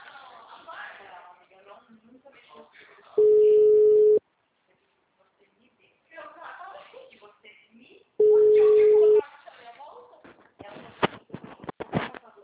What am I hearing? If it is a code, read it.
Ringback tone (Europe, 425 Hz), with speech.